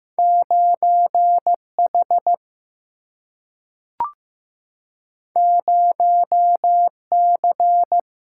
Morse code, heard as a keyed tone, about 15 wpm, 700 Hz. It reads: 9H 0C
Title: Single-Letter Single-Number 15wpm